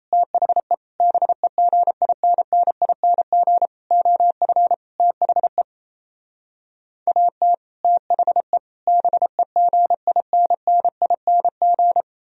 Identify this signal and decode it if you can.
Morse code, heard as a keyed tone, about 37 wpm, 700 Hz. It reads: THE BEGINNING OF THE AT THE BEGINNING